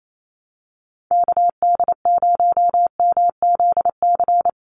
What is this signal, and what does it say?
Morse code, heard as a keyed tone, about 28 wpm, 700 Hz. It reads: KD0MZC
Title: USA Call Signs 28wpm